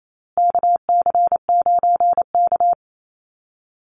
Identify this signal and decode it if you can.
Morse code, heard as a keyed tone, about 28 wpm, 700 Hz. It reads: KC9K
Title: Call Signs Repeated 2x - 28wpm